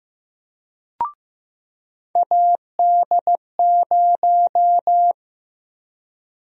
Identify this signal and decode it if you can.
Morse code, heard as a keyed tone, about 15 wpm, 700 Hz. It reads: AD0
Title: USA Call Sign Prefixes - 15wpm